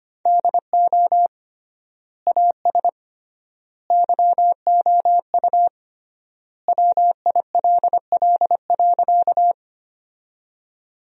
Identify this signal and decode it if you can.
Morse code, heard as a keyed tone, about 25 wpm, 700 Hz. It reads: DO AS YOU WILL.
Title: 3x Word-Spacing - Sentences from Top 100 Words - 25wpm